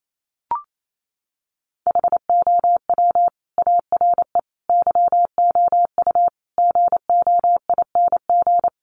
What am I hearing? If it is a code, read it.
Morse code, heard as a keyed tone, about 28 wpm, 700 Hz. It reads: HOW ARE YOU GOING
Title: Sets of 5 Words 28wpm